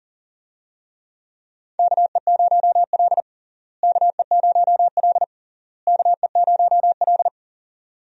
Morse code, heard as a keyed tone, about 40 wpm, 700 Hz. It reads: KE0L KE0L KE0L
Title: USA Call Signs - 4 Characters - Repeated 3x - 40wpm